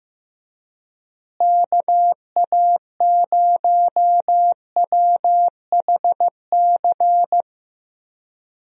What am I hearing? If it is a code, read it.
Morse code, heard as a keyed tone, about 15 wpm, 700 Hz. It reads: KA0WHC
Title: USA Call Signs - 6 Characters - Repeated 2x - 15wpm